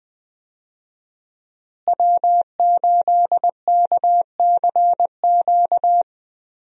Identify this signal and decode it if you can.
Morse code, heard as a keyed tone, about 20 wpm, 700 Hz. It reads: W8KCQ
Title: USA Call Signs - 5 Characters - Repeated 2x - 20wpm